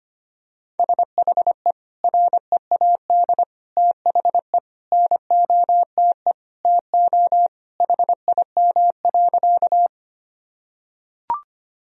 Morse code, heard as a keyed tone, about 25 wpm, 700 Hz. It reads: SHE READ THE NOTE TO HIM.
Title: Sentences from Top 1000 Words - Review Part 2 - 25wpm